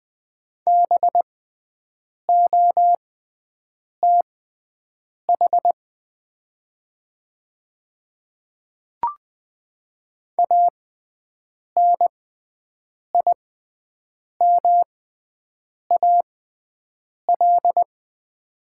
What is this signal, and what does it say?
Morse code, heard as a keyed tone, about 20 wpm, spaced as for 7 wpm, 700 Hz. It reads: BOTH ANIMAL
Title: 6x Character Spacing - Top 300 Words - 20wpm